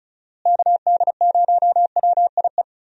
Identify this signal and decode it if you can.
Morse code, heard as a keyed tone, about 34 wpm, 700 Hz. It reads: KD0WIE
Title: USA Call Signs - 6 Characters - Repeated 2x - 35wpm